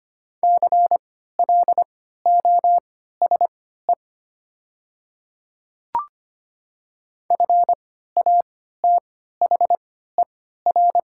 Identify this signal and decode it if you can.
Morse code, heard as a keyed tone, about 25 wpm, spaced as for 14 wpm, 700 Hz. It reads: CLOSE FATHER
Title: 3x Character Spacing - Top 500 Words - 25wpm